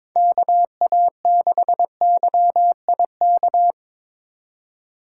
Morse code, heard as a keyed tone, about 26 wpm, 700 Hz. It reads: KA6YIK
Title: Speed Racing - USA 2x3 Call Signs 22wpm